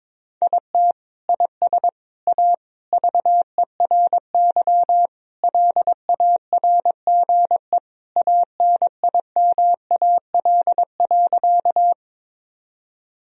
Morse code, heard as a keyed tone, about 22 wpm, 700 Hz. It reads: IT IS A VERY LARGE ANIMAL.
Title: Sentences from Top 1000 Words - Review Part 1 - 22wpm